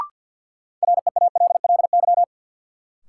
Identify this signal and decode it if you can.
Morse code, heard as a keyed tone, about 53 wpm, 700 Hz. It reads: REALLY
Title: Sets of 4 Words - New Word Drill - 50wpm